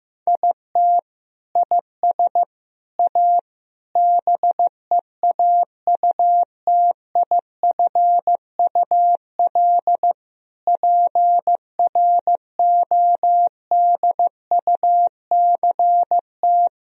Morse code, heard as a keyed tone, about 16 wpm, 700 Hz. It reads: IT IS A BEAUTIFUL PRODUCT
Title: Sentences from Top 1000 Words - Review Part 2 - 15wpm